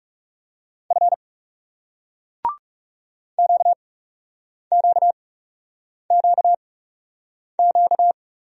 Morse code, heard as a keyed tone, about 40 wpm, spaced as for 8 wpm, 700 Hz. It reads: F QQQQ